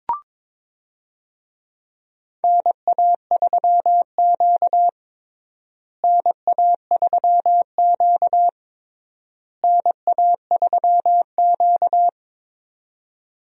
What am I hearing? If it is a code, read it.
Morse code, heard as a keyed tone, about 22 wpm, 700 Hz. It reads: NA3Q NA3Q NA3Q
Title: USA Call Signs - 4 Characters - Repeated 3x - 22wpm